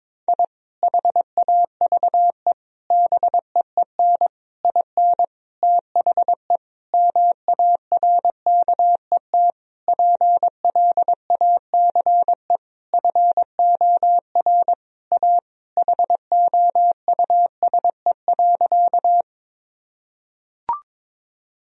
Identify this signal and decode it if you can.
Morse code, heard as a keyed tone, about 22 wpm, 700 Hz. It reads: I HAVE BEEN IN THE MARKET PLACE FOR A HOUSE.
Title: Sentences from Top 1000 Words - Review Part 3 - 22wpm